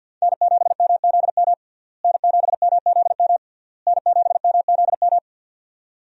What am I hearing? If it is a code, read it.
Morse code, heard as a keyed tone, about 51 wpm, 700 Hz. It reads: N7MZM N7MZM N7MZM
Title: USA Call Signs - 5 Characters - Repeated 3x - 50wpm